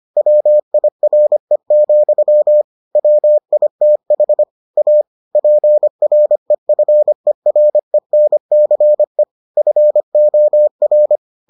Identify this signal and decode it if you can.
Morse code, heard as a keyed tone, about 25 wpm, 600 Hz. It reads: WIRE, WITH A PREFERENCE FOR